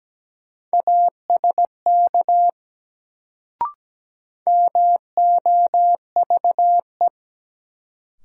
Morse code, heard as a keyed tone, about 17 wpm, 700 Hz. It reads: ASK MOVE
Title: Sentences from Top 200 Words - New Word Drill - 17wpm